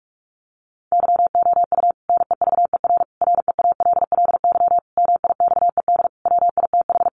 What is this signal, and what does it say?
Morse code, heard as a keyed tone, about 44 wpm, 700 Hz. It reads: YOU NEVER REALLY MIXED WITH
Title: Sentences from Top 1500 Words 45wpm